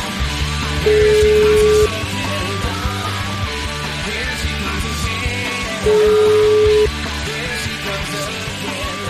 Ringback tone (Europe, 425 Hz), with noise.